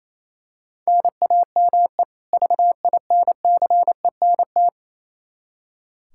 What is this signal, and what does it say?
Morse code, heard as a keyed tone, about 27 wpm, 700 Hz. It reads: NAME VINCENT
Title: Practice QSO - Exchange 1 - Beginning - 28wpm